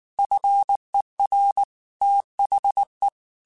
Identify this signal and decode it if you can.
Morse code, heard as a keyed tone, about 19 wpm, 800 Hz. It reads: FER THE
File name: QSOs / QSOs-mp3s / kc7sac@18